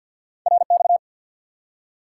Morse code, heard as a keyed tone, about 49 wpm, 700 Hz. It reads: RX
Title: Top 90 QSO Elements 50wpm